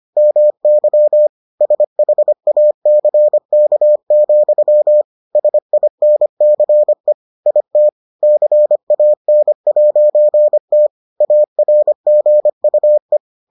Morse code, heard as a keyed tone, about 25 wpm, 600 Hz. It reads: MY SHACK, SINCE IT CAN'T ARGUE